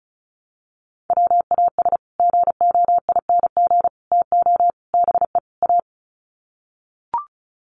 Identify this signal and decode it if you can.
Morse code, heard as a keyed tone, about 34 wpm, 700 Hz. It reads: WAS GOING TO BE A
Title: Sets of 5 Words 35wpm